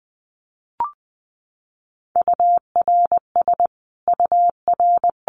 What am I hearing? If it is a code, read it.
Morse code, heard as a keyed tone, about 20 wpm, 700 Hz. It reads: URS UR
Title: Abbreviations Repeated 3x - 20wpm